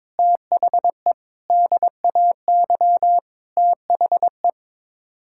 Morse code, heard as a keyed tone, about 22 wpm, 700 Hz. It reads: THE DAY THE
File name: practice / 22wpm / Sets of 8 Words - 22wpm